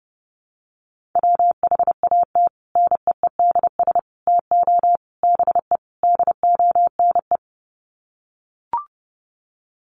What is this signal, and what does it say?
Morse code, heard as a keyed tone, about 30 wpm, 700 Hz. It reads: WHAT NEEDS TO BE DONE
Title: Sets of 5 Words Encore 30wpm